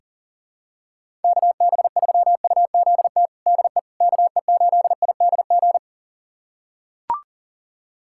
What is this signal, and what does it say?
Morse code, heard as a keyed tone, about 40 wpm, 700 Hz. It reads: KB3UZT DE KE8IDG
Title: USA Callsign DE Callsign 40wpm